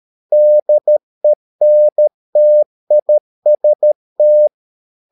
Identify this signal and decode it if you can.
Morse code, heard as a keyed tone, about 13 wpm, 600 Hz. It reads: DENTIST
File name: mp3 files / Lexiphiles - 13wpm